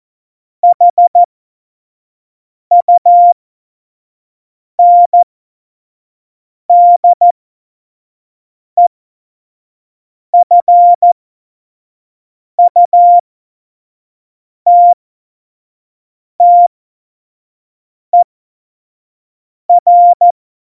Morse code, heard as a keyed tone, about 14 wpm, spaced as for 5 wpm, 700 Hz. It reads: HUNDEFUTTER
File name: Morsekurs / Witze / Witz 18. Dezember 2018, 25 F70